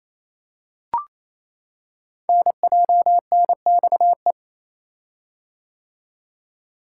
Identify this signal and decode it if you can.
Morse code, heard as a keyed tone, about 29 wpm, 700 Hz. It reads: NJNXE
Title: Code Groups - 5 Letters - 28wpm